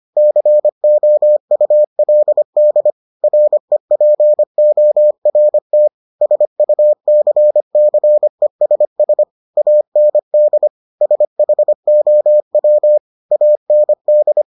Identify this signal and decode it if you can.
Morse code, heard as a keyed tone, about 25 wpm, 600 Hz. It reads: COULD REPORT SUCCESS AND SHOW AND